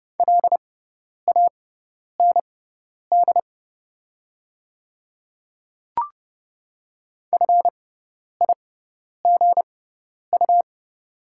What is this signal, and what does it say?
Morse code, heard as a keyed tone, about 30 wpm, spaced as for 10 wpm, 700 Hz. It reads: LAND FIGU
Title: 6x Character Spacing - Top 500 Words - 30wpm